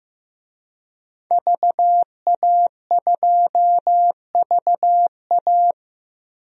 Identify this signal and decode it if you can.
Morse code, heard as a keyed tone, about 16 wpm, 700 Hz. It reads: VA2VA
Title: DXCC-100 Call Signs - 5 Characters - 15wpm